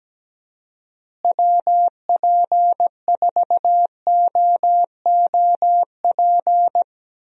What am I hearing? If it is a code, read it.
Morse code, heard as a keyed tone, about 17 wpm, 700 Hz. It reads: WP4OOP
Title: USA Call Signs - 6 Characters - Repeated 3x - 17wpm